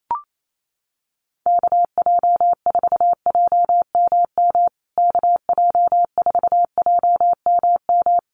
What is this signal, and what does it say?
Morse code, heard as a keyed tone, about 28 wpm, 700 Hz. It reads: KJ4JMM KJ4JMM
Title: USA Call Signs Repeated 3x 28wpm